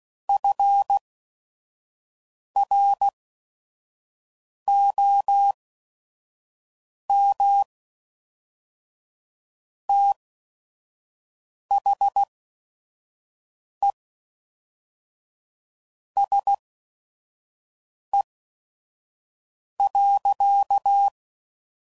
Morse code, heard as a keyed tone, about 16 wpm, spaced as for 5 wpm, 800 Hz. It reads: FROM THE SE.
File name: CW Code Groups @5WPM / Weather Sets / Weather 4 17